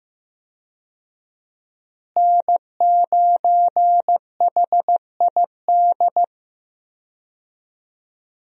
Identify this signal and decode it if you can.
Morse code, heard as a keyed tone, about 15 wpm, 700 Hz. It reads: N9HID
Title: USA Call Signs - 5 Characters - Repeated 2x - 15wpm